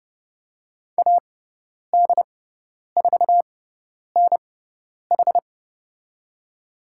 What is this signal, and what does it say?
Morse code, heard as a keyed tone, about 30 wpm, spaced as for 10 wpm, 700 Hz. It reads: AD4NH
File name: Lesson 16g - Letter c - Call Signs 30wpm